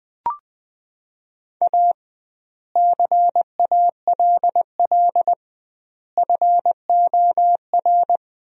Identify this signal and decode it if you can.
Morse code, heard as a keyed tone, about 21 wpm, 700 Hz. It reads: A CALL FOR
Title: 2x Word-Spacing - Sentences from Top 300 Words - 20wpm